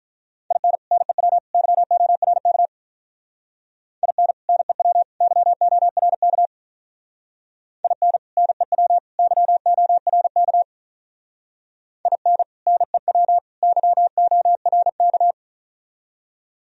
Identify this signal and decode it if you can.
Morse code, heard as a keyed tone, about 44 wpm, 700 Hz. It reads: IN NEW YORK IN NEW YORK IN NEW YORK IN NEW YORK